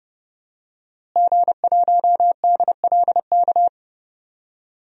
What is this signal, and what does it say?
Morse code, heard as a keyed tone, about 30 wpm, 700 Hz. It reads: G1DLK
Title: DXCC-100 Call Signs - 5 Characters - Repeated 2x - 30wpm